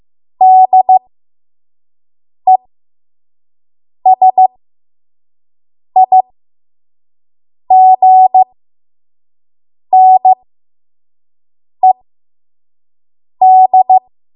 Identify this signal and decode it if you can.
Morse code, heard as a keyed tone, about 15 wpm, spaced as for 5 wpm, 750 Hz. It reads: DESIGNED
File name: Morse / 05WPM